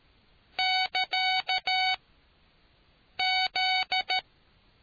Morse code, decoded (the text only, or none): <KA>Z